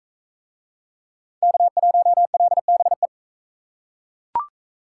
Morse code, heard as a keyed tone, about 41 wpm, 700 Hz. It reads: K1LBE